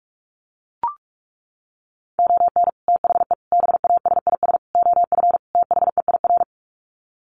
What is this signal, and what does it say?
Morse code, heard as a keyed tone, about 44 wpm, 700 Hz. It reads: ON THE BASIS OF THEIR
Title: Sets of 5 Words Encore 45wpm